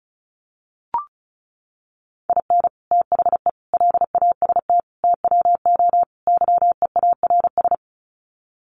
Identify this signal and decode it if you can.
Morse code, heard as a keyed tone, about 34 wpm, 700 Hz. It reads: IN THE LAST TWO YEARS
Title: Sets of 5 Words 35wpm